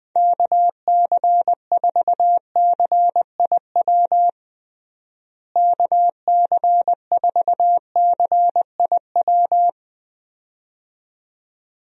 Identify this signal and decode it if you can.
Morse code, heard as a keyed tone, about 21 wpm, 700 Hz. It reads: KC4CIW KC4CIW
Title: USA Call Signs - 6 Characters - Repeated 2x - 20wpm